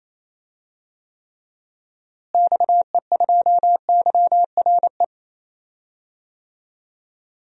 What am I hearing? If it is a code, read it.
Morse code, heard as a keyed tone, about 28 wpm, 700 Hz. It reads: XE2YRE
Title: DXCC-100 Call Signs - 6 Characters - Repeated 2x - 28wpm